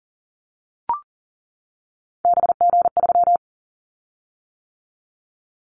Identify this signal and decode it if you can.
Morse code, heard as a keyed tone, about 37 wpm, 690 Hz. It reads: BG3
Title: Callsign Prefixes for Top 100 DXCC - 40wpm